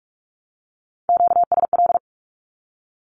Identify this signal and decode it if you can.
Morse code, heard as a keyed tone, about 46 wpm, 700 Hz. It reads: QSL
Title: Q-Codes with Spoken Meaning - 45wpm